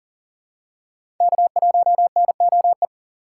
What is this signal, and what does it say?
Morse code, heard as a keyed tone, about 40 wpm, 700 Hz. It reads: K1NOE
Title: USA Call Signs - 5 Characters - 2x Character Spacing - 40wpm